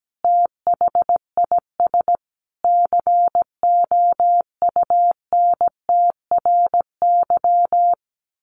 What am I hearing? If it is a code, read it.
Morse code, heard as a keyed tone, about 17 wpm, 700 Hz. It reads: THIS COUNTRY
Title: Sentences from Top 1200 Words 17wpm